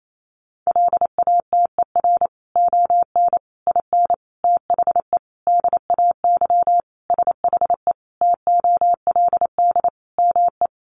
Morse code, heard as a keyed tone, about 28 wpm, 700 Hz. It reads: LATER ON IN THE DAY SHE TOLD ME
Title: Sentences from Top 700 Words 28wpm